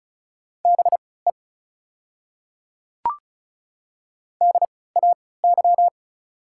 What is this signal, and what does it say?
Morse code, heard as a keyed tone, about 36 wpm, spaced as for 20 wpm, 700 Hz. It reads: BE DAY
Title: 3x Character Spacing - Top 100 Words - 35wpm